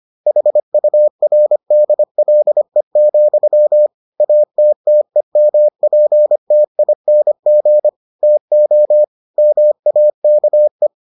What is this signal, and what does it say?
Morse code, heard as a keyed tone, about 25 wpm, 600 Hz. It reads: HURDLE, ATTEMPTING TO MAKE